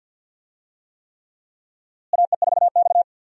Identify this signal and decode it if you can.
Morse code, heard as a keyed tone, about 48 wpm, 700 Hz. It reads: AE4X
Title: USA Call Signs - 4 Characters - Repeated 3x - 50wpm